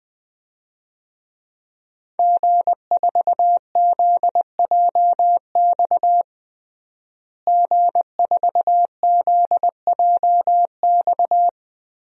Morse code, heard as a keyed tone, about 20 wpm, 700 Hz. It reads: G4ZJX G4ZJX
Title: DXCC-100 Call Signs - 5 Characters - Repeated 3x - 20wpm